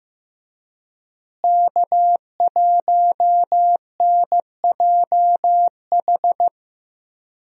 Morse code, heard as a keyed tone, about 15 wpm, 700 Hz. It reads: K1NJH
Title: USA Call Signs - 5 Characters - Repeated 3x - 15wpm